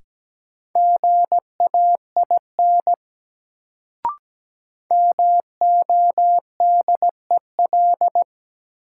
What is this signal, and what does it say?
Morse code, heard as a keyed tone, about 17 wpm, 700 Hz. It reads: GAIN MODEL
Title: Sentences from Top 1000 Words - New Word Drill - 17wpm